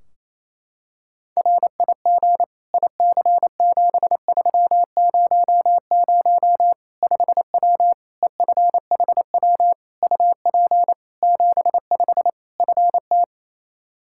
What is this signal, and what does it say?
Morse code, heard as a keyed tone, about 28 wpm, 700 Hz. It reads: RIG IC7300 5W EFHW UP 75 FT
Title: Practice QSO - Exchange 2 - Info - 28wpm